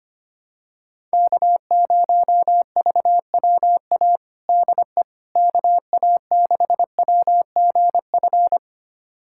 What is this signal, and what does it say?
Morse code, heard as a keyed tone, about 25 wpm, 700 Hz. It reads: K0VWA DE KA6WGF
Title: USA Callsign DE Callsign 25wpm